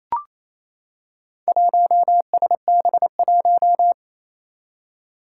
Morse code, heard as a keyed tone, about 28 wpm, 700 Hz. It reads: 1SB1
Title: Four Challenging Letter-Number Combinations 28wpm